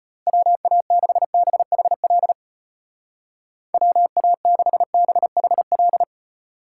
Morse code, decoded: WA6BHL WA6BHL

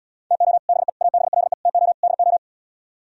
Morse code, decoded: TO BE ABLE TO DO

